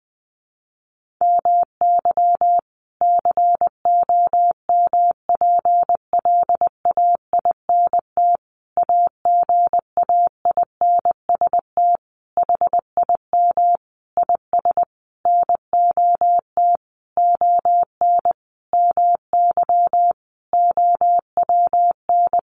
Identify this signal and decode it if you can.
Morse code, heard as a keyed tone, about 19 wpm, 700 Hz. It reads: MY COMPLAINT AGAINST HIM IS NOT ON MY OWN
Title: Sentences from Top 1400 Words 20wpm